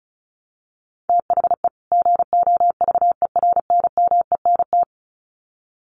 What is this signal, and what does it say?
Morse code, heard as a keyed tone, about 34 wpm, 700 Hz. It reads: THE GOVERNMENT
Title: Sets of 2 Words 35wpm